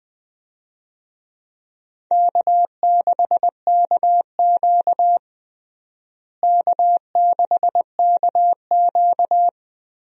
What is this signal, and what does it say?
Morse code, heard as a keyed tone, about 20 wpm, 700 Hz. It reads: K6KQ K6KQ
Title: USA Call Signs - 4 Characters - Repeated 2x - 20wpm